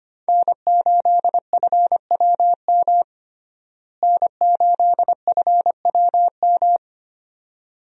Morse code, heard as a keyed tone, about 25 wpm, 700 Hz. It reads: N8FWM N8FWM
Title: USA Call Signs - 5 Characters - Repeated 2x - 25wpm